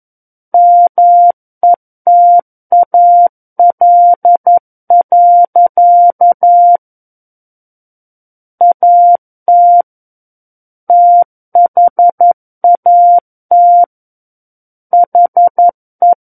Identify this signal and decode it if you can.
Morse code, heard as a keyed tone, about 11 wpm, 700 Hz. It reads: METAL. AT THAT HE
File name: 11 WPM